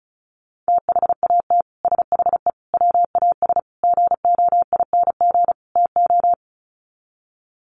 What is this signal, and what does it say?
Morse code, heard as a keyed tone, about 34 wpm, 700 Hz. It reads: THAT SHE WAS GOING TO
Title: Sets of 5 Words Encore2 35wpm